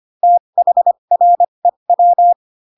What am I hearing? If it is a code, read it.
Morse code, heard as a keyed tone, about 25 wpm, spaced as for 22 wpm, 700 Hz. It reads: THREW
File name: Pinocchio / Pinocchio ch7 18 25 ik1hge